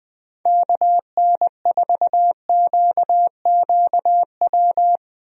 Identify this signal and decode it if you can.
Morse code, heard as a keyed tone, about 20 wpm, 700 Hz. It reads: KN4QQW
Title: USA Call Signs - 6 Characters - Repeated 2x - 20wpm